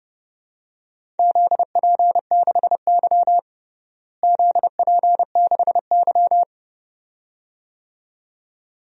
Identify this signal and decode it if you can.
Morse code, heard as a keyed tone, about 30 wpm, 700 Hz. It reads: ZP6Y ZP6Y
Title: DXCC-100 Call Signs - 4 Characters - Repeated 2x - 30wpm